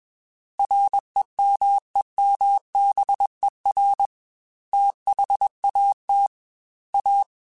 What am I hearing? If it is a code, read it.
Morse code, heard as a keyed tone, about 21 wpm, 800 Hz. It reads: REMEMBER THAT A
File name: Technical Subjects / Technical Subjects-mp3s / Sw Pwr Sup @20